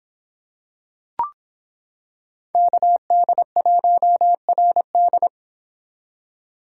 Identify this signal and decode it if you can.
Morse code, heard as a keyed tone, about 26 wpm, 700 Hz. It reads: KD1RD